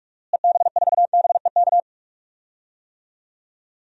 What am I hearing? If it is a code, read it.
Morse code, heard as a keyed tone, about 43 wpm, 700 Hz. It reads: EB4BEK
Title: DXCC-100 Call Signs - 6 Characters - Repeated 3x - 45wpm